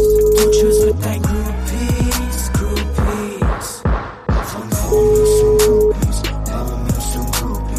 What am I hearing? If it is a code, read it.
Ringback tone (Europe, 425 Hz), with noise.